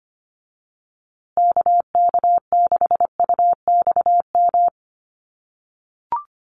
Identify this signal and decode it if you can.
Morse code, heard as a keyed tone, about 25 wpm, 700 Hz. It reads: KK6UXM
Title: USA Call Signs Repeated 2x 25wpm